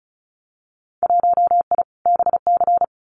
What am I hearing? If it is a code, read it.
Morse code, heard as a keyed tone, about 34 wpm, 700 Hz. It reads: 1I BC
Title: Winter Field Day Exchanges Encore 35wpm